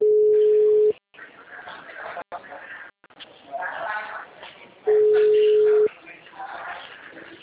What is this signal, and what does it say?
Ringback tone (Europe, 425 Hz), with speech.
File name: out- -2015- - - .762